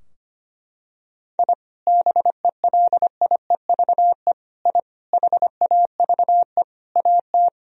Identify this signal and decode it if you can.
Morse code, heard as a keyed tone, about 25 wpm, 700 Hz. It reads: I BELIEVE I HAVE AT
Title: Sentences from Top 1000 Words - Review Part 2 - 25wpm